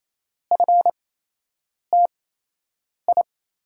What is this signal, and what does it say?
Morse code, heard as a keyed tone, about 28 wpm, spaced as for 8 wpm, 700 Hz. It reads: FTI
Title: USA Call Signs - 5 Characters - 8x Character Spacing - 28wpm